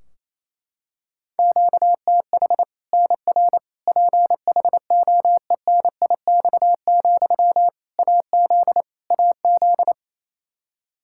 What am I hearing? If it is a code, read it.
Morse code, heard as a keyed tone, about 28 wpm, 700 Hz. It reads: QTH NR PHOENIX, AZ AZ
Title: Practice QSO - Exchange 1 - Beginning - 28wpm